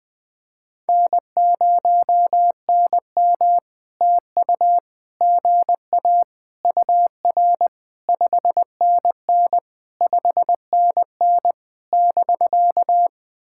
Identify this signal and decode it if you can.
Morse code, heard as a keyed tone, about 20 wpm, 700 Hz. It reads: N0NM TU GA UR 5NN 5NN <BK>
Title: POTA QSO Practice - 20wpm